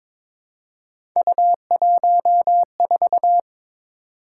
Morse code, heard as a keyed tone, about 22 wpm, 700 Hz. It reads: U14
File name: practice / 22wpm / Three Letter-Number Combinations - 22wpm